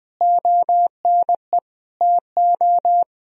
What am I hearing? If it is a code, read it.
Morse code, heard as a keyed tone, about 21 wpm, 700 Hz. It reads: ONE TO
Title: Sentences from Top 1000 Words - Review Part 4 - 20wpm